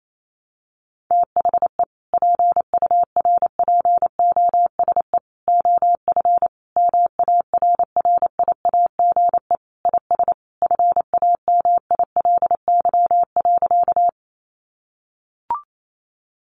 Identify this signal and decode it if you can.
Morse code, heard as a keyed tone, about 28 wpm, 700 Hz. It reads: THE PURPOSE OF MARRIAGE IS FAMILY.
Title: Sentences from Top 800 Words 28wpm